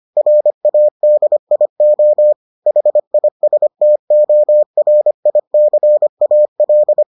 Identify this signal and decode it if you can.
Morse code, heard as a keyed tone, about 25 wpm, 600 Hz. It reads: RADIO HISTORICAL